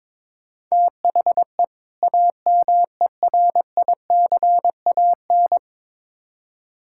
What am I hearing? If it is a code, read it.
Morse code, heard as a keyed tone, about 21 wpm, 700 Hz. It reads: THE AMERICAN
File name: Build A Phrase - 2-Word Phrases - 22wpm